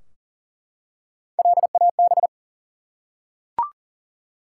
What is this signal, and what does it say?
Morse code, heard as a keyed tone, about 40 wpm, 700 Hz. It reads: LAB
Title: Three Letter Words - 40wpm